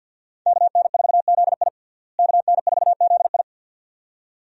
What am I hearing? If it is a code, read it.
Morse code, heard as a keyed tone, about 53 wpm, 700 Hz. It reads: KN4ZI KN4ZI
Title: USA Call Signs - 5 Characters - Repeated 2x - 50wpm